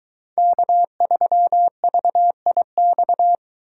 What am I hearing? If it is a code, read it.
Morse code, heard as a keyed tone, about 22 wpm, 700 Hz. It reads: K3VIX